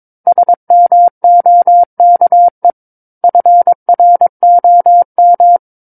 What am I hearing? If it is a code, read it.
Morse code, heard as a keyed tone, about 22 wpm, 700 Hz. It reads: SMOKE FROM
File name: 06-Chapter-06 22 WPM